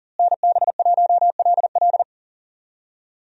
Morse code, heard as a keyed tone, about 37 wpm, 700 Hz. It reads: NB1LL
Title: USA Call Signs - 5 Characters - Repeated 3x - 40wpm